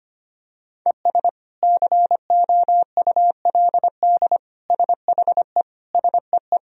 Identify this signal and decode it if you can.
Morse code, heard as a keyed tone, about 25 wpm, 700 Hz. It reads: ES COULD SHE SEE
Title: Sentences from Top 100 Words and Abbreviations 25wpm